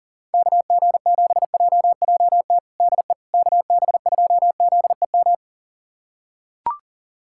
Morse code, heard as a keyed tone, about 40 wpm, 700 Hz. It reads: KG7JJT DE KB2ZEM
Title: USA Callsign DE Callsign - 6 Characters - 40wpm